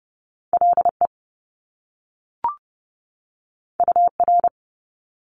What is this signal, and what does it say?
Morse code, heard as a keyed tone, about 30 wpm, 700 Hz. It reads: LE UR
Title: Top Two Letter Combinations in Words 30wpm